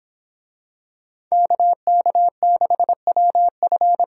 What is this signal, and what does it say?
Morse code, heard as a keyed tone, about 24 wpm, 700 Hz. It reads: KK6WF